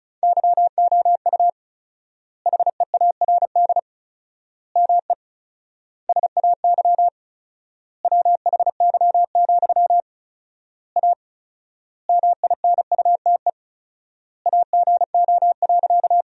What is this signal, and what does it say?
Morse code, heard as a keyed tone, about 36 wpm, 700 Hz. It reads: YOU HEARD ME SAY WHY, A MINUTE AGO.
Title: 4x Word-Spacing - Sentences from Top 500 Words - 35wpm